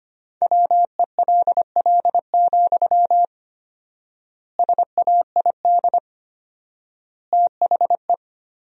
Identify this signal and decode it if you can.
Morse code, heard as a keyed tone, about 25 wpm, 700 Hz. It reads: WELL, SAID THE
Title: 4x Word-Spacing - Sentences from Top 400 Words - 25wpm